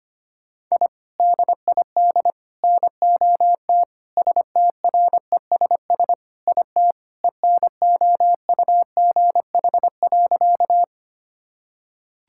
Morse code, heard as a keyed tone, about 25 wpm, 700 Hz. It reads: I DID NOT STRESS IT ENOUGH.
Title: Sentences from Top 1000 Words - Review Part 6 - 25wpm